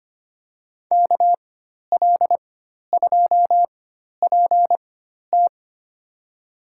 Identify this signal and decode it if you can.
Morse code, heard as a keyed tone, about 25 wpm, spaced as for 12 wpm, 700 Hz. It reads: KL2PT
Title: USA Call Signs - 5 Characters - 4x Character Spacing - 25wpm